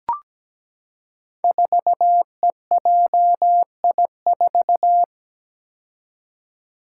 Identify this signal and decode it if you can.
Morse code, heard as a keyed tone, about 17 wpm, 700 Hz. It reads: 4EJI4
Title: Five Challenging Letter-Number Combinations 17wpm